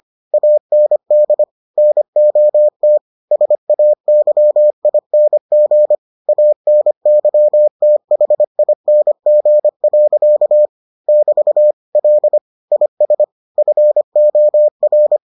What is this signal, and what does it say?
Morse code, heard as a keyed tone, about 25 wpm, 600 Hz. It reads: AND NOT SAYING ANYTHING. <BT> L IS FOR